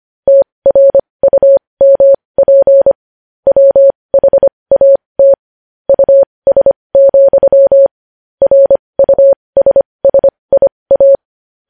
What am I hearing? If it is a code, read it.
Morse code, heard as a keyed tone, about 25 wpm, spaced as for 20 wpm, 550 Hz. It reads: TRUMP WHAT US, RUSSIA
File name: podcast / News headlines-1x-25x20